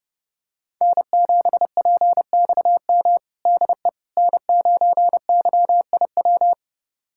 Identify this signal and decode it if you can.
Morse code, heard as a keyed tone, about 30 wpm, 700 Hz. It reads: N7PXM DE N9YIW
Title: USA Callsign DE Callsign - 5 Characters - 30wpm